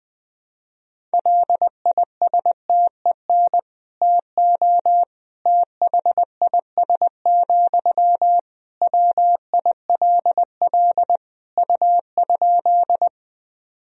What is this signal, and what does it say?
Morse code, heard as a keyed tone, about 20 wpm, 700 Hz. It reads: LISTEN TO THIS, WILL U?
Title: Sentences from Top 400 Words and Abbreviations 20wpm